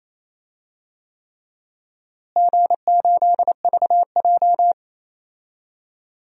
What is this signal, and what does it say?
Morse code, heard as a keyed tone, about 28 wpm, 700 Hz. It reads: G8VJ
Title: DXCC-100 Call Signs - 4 Characters - Repeated 3x - 28wpm